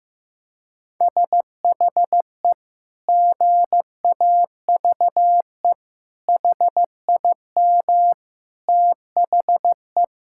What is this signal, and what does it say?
Morse code, heard as a keyed tone, about 16 wpm, 700 Hz. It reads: SHE GAVE HIM THE
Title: Sentences from Top 1600 Words 15wpm